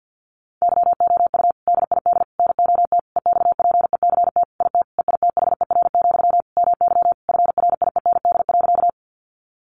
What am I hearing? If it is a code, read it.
Morse code, heard as a keyed tone, about 50 wpm, 700 Hz. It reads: YOU DID NOT EXPECT IT EITHER, MY FRIEND.
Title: Sentences from Top 1100 Words 50wpm